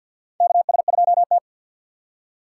Morse code, heard as a keyed tone, about 46 wpm, 700 Hz. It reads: KS2T
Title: USA Call Signs - 3x Character Spacing - 50wpm